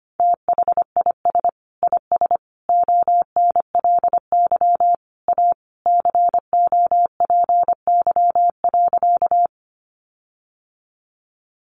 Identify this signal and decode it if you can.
Morse code, heard as a keyed tone, about 25 wpm, 700 Hz. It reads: THIS IS ONLY A COPY.
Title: Sentences from Top 1400 Words 25wpm